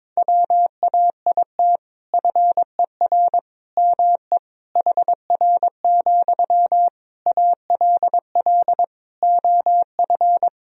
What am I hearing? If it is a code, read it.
Morse code, heard as a keyed tone, about 22 wpm, 700 Hz. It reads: WAIT FER ME HR, ALL OF
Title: Sentences from Top 500 Words and Abbreviations 22wpm